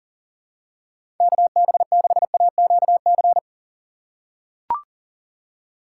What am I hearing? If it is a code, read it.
Morse code, heard as a keyed tone, about 40 wpm, 700 Hz. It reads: KB6AQC
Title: USA Call Signs - 6 Characters - 6x Character Spacing - 40wpm